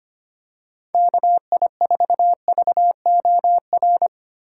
Morse code, heard as a keyed tone, about 25 wpm, 700 Hz. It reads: KI4VOR
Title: USA Call Signs - 6 Characters - 3x Character Spacing - 25wpm